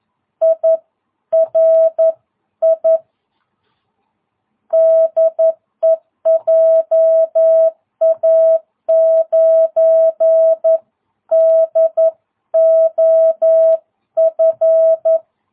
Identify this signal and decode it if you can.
Morse code, heard as a keyed tone, about 11 wpm, 660 Hz. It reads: IRI DE JA9DOF
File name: jh8cbh / 2022cw